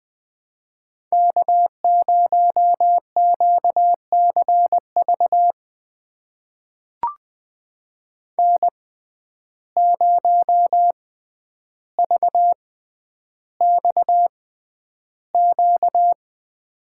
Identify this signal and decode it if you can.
Morse code, heard as a keyed tone, about 20 wpm, spaced as for 10 wpm, 700 Hz. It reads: K0QCV N0VXQ
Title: USA Call Signs - 5 Characters - 6x Character Spacing - 20wpm